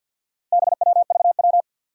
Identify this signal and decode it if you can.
Morse code, heard as a keyed tone, about 52 wpm, 700 Hz. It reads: BWVW
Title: Four Challenging Letter-Number Combinations 50wpm